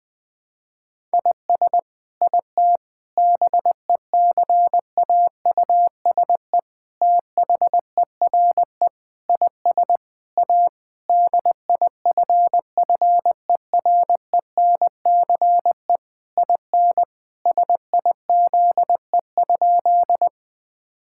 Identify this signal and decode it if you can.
Morse code, heard as a keyed tone, about 21 wpm, 700 Hz. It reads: IS IT BECAUSE THERE IS A DIFFERENCE IN SIZE?
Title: Sentences from Top 1000 Words - Review Part 5 - 20wpm